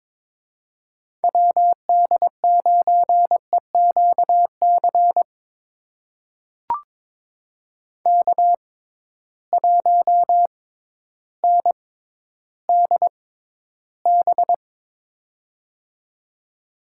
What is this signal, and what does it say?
Morse code, heard as a keyed tone, about 21 wpm, spaced as for 12 wpm, 700 Hz. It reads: WD9EQC K1NDB
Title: USA Call Signs - 6x Character Spacing - 22wpm